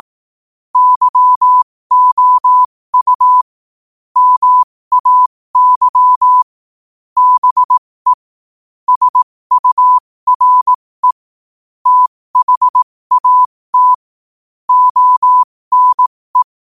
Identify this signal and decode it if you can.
Morse code, heard as a keyed tone, about 18 wpm, spaced as for 15 wpm, 1000 Hz. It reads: YOU MAY BE SURE THAT ONE
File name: qotdfiles / 15wpm